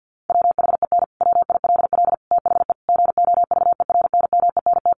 Morse code, heard as a keyed tone, about 50 wpm, 700 Hz. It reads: WHEN WILL THE GOVERNMENT
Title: Sentences from Top 1400 Words 50wpm